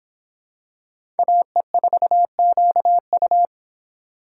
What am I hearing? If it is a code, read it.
Morse code, heard as a keyed tone, about 26 wpm, 700 Hz. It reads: AE4QU